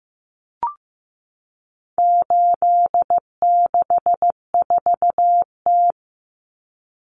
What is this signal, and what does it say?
Morse code, heard as a keyed tone, about 15 wpm, 700 Hz. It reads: 864T
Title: Common Cut Numbers - 4-Digit 15wpm